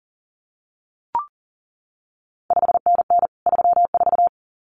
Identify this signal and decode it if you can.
Morse code, heard as a keyed tone, about 40 wpm, 700 Hz. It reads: 5NN 34
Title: CQ WW DX Contest Exchange Encore - 40wpm